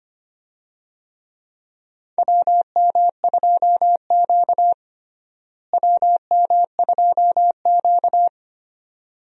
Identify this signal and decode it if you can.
Morse code, heard as a keyed tone, about 25 wpm, 700 Hz. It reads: WM2Q WM2Q
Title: USA Call Signs - 4 Characters - Repeated 2x - 25wpm